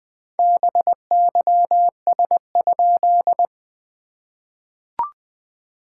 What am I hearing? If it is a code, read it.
Morse code, heard as a keyed tone, about 20 wpm, 690 Hz. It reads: BYS?
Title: SOTA QSO Practice 20wpm